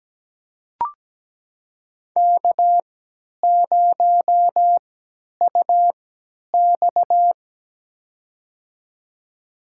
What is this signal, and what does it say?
Morse code, heard as a keyed tone, about 17 wpm, spaced as for 10 wpm, 700 Hz. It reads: K0UX
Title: USA Call Signs - 3x Character Spacing - 17wpm